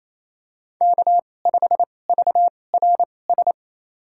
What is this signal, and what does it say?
Morse code, heard as a keyed tone, about 28 wpm, spaced as for 20 wpm, 700 Hz. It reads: K5VRS
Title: USA Call Signs - 2x Character Spacing - 28wpm